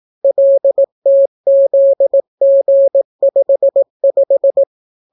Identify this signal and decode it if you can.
Morse code, heard as a keyed tone, about 18 wpm, 550 Hz. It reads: LTZG55